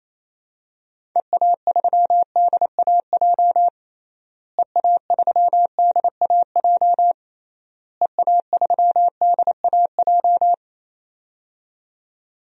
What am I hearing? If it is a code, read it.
Morse code, heard as a keyed tone, about 28 wpm, 700 Hz. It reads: EA3DAJ EA3DAJ EA3DAJ
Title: DXCC-100 Call Signs - 6 Characters - Repeated 3x - 28wpm